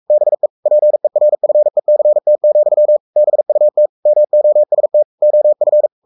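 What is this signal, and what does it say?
Morse code, heard as a keyed tone, about 44 wpm, 600 Hz. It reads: BE PERFECT, BUT MOST OF
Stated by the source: Morse code at 43 wpm sending Ken Follett's novel Never in English, from CW med Gnister.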